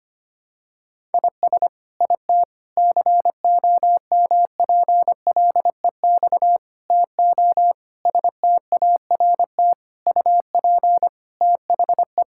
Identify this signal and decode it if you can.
Morse code, heard as a keyed tone, about 25 wpm, 700 Hz. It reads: IS IT COMPLEX TO START UP THE
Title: Sentences from Top 1000 Words - Review Part 6 - 25wpm